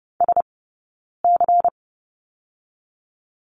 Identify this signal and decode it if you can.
Morse code, heard as a keyed tone, about 30 wpm, 700 Hz. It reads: S C
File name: Lesson 17d - Letter u - Two Characters 30wpm